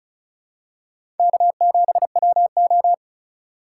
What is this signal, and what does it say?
Morse code, heard as a keyed tone, about 34 wpm, 700 Hz. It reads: K7WO
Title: USA Call Signs - 3x Character Spacing - 35wpm